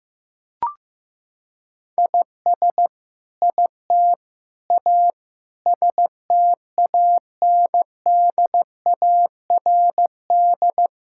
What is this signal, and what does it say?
Morse code, heard as a keyed tone, about 14 wpm, 700 Hz. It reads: IS IT A STANDARD
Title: Sentences from Top 1000 Words - Review Part 6 - 15wpm